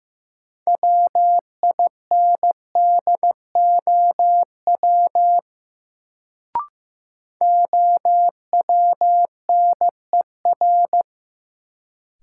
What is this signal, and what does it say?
Morse code, heard as a keyed tone, about 15 wpm, 700 Hz. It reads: WINDOW OWNER
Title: Sentences from Top 1600 Words - New Word Drill - 15wpm